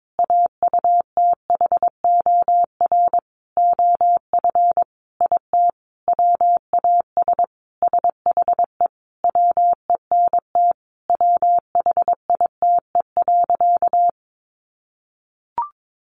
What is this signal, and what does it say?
Morse code, heard as a keyed tone, about 22 wpm, 700 Hz. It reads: AUTHOR OF IT WAS SHE WENT WHITE.
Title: Sentences from Top 1200 Words 22wpm